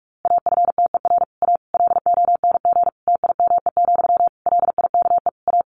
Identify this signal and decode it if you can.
Morse code, heard as a keyed tone, about 44 wpm, 700 Hz. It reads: AFTER A LONG TIME, LIKE A
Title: A Sound Of Thunder 45wpm